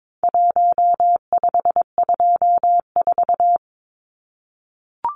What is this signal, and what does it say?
Morse code, heard as a keyed tone, about 22 wpm, 700 Hz. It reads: 1524
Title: Numbers - 4-Digit 22wpm